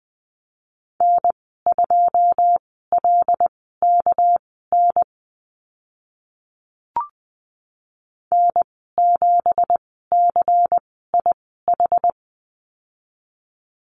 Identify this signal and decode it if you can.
Morse code, heard as a keyed tone, about 20 wpm, spaced as for 14 wpm, 700 Hz. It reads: N2LKN N7CIH
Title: 200 Lesson 32g - Letter k - Call Signs 20wpm